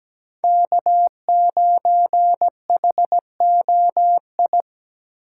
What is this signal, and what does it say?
Morse code, heard as a keyed tone, about 17 wpm, 700 Hz. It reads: K9HOI
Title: USA Call Signs - 5 Characters - 2x Character Spacing - 17wpm